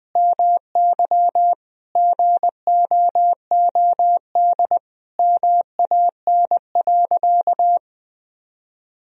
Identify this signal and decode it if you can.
Morse code, heard as a keyed tone, about 21 wpm, 700 Hz. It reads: MY GOOD MAN.
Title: Sentences from Top 1000 Words - Review Part 1 - 20wpm